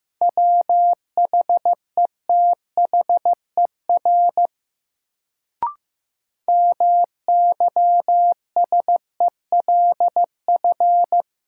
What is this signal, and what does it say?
Morse code, heard as a keyed tone, about 15 wpm, 700 Hz. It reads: WHETHER MYSELF
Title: Sentences from Top 1300 Words - New Word Drill Alt - 15wpm